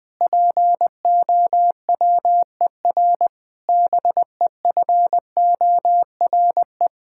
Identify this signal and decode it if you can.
Morse code, heard as a keyed tone, about 20 wpm, 700 Hz. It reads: POWER BEFORE
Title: Sentences from Top 1000 Words - Review Part 6 - 20wpm